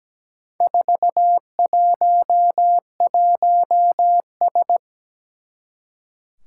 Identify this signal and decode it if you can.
Morse code, heard as a keyed tone, about 17 wpm, 700 Hz. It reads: 411S
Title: Four Challenging Letter-Number Combinations 17wpm